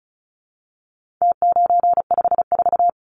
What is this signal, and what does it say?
Morse code, heard as a keyed tone, about 34 wpm, 700 Hz. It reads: T954
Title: Numbers - Contest Style - 4-Digit 35wpm